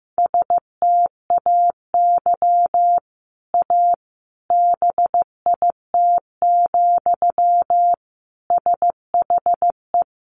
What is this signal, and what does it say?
Morse code, heard as a keyed tone, about 15 wpm, 700 Hz. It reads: STAY A BIT, SHE
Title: Sentences from Top 900 Words 15wpm